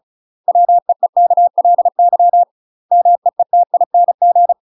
Morse code, heard as a keyed tone, about 35 wpm, 700 Hz. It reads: WEEKLY MEETING
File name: UK MASTER QSOs TRAINING / 35WPM / UK MASTER QSOs 0007